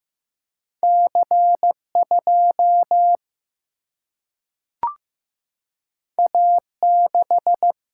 Morse code, heard as a keyed tone, about 15 wpm, 700 Hz. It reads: C 2 A 6
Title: Single-Letter Single-Number 15wpm